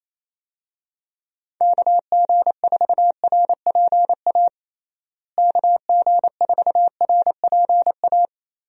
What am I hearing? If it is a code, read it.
Morse code, heard as a keyed tone, about 27 wpm, 700 Hz. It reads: KG4RPA KG4RPA
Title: USA Call Signs - 6 Characters - Repeated 3x - 28wpm